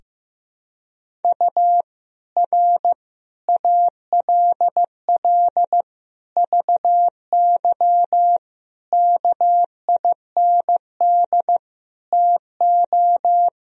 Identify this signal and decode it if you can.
Morse code, heard as a keyed tone, about 15 wpm, 700 Hz. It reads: U R ALL VY KIND TO
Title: Sentences from Top 200 Words and Abbreviations 15wpm